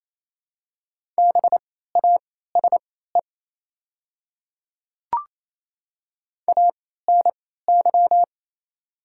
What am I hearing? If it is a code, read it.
Morse code, heard as a keyed tone, about 28 wpm, spaced as for 16 wpm, 700 Hz. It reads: BASE ANY
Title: 3x Character Spacing - Top 300 Words - 28wpm